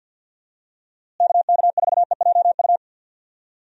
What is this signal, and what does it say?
Morse code, heard as a keyed tone, about 50 wpm, 700 Hz. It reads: KK4EJU
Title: USA Call Signs - 6 Characters - Repeated 3x - 50wpm